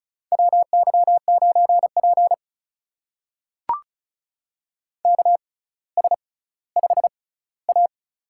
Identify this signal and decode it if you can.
Morse code, heard as a keyed tone, about 36 wpm, spaced as for 18 wpm, 700 Hz. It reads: WY9P KS5A
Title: USA Call Signs - 4 Characters - 6x Character Spacing - 35wpm